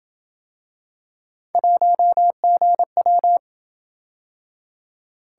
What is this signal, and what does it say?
Morse code, heard as a keyed tone, about 27 wpm, 700 Hz. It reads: 1GW